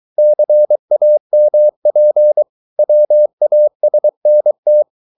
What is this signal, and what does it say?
Morse code, heard as a keyed tone, about 23 wpm, 600 Hz. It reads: CAMP WASNT